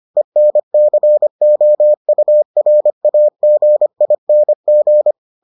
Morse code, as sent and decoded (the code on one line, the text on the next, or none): . -. -.-. --- ..- .-. .- --. .. -. --.
ENCOURAGING